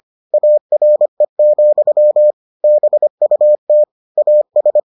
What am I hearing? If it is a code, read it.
Morse code, heard as a keyed tone, about 25 wpm, 600 Hz. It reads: ARE, BUT AS